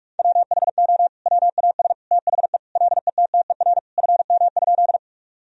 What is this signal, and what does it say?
Morse code, heard as a keyed tone, about 46 wpm, 700 Hz. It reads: WHO WAS THE LETTER FM?
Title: Sentences from Top 300 Words and Abbreviations 45wpm